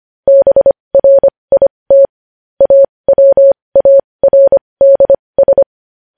Morse code, heard as a keyed tone, about 25 wpm, spaced as for 20 wpm, 550 Hz. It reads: BRIT AWARDS
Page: News Headlines in Morse Code at 20 WPM